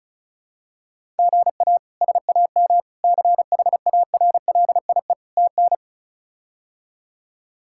Morse code, heard as a keyed tone, about 36 wpm, 700 Hz. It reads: GA SAM CHARLIE TN
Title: K1USN SST QSO Practice - 35wpm